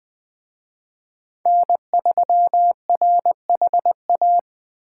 Morse code, heard as a keyed tone, about 20 wpm, 700 Hz. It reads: N3RHA